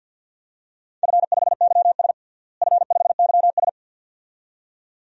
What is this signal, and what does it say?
Morse code, heard as a keyed tone, about 50 wpm, 700 Hz. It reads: F5YS F5YS
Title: DXCC-100 Call Signs - 4 Characters - Repeated 2x - 50wpm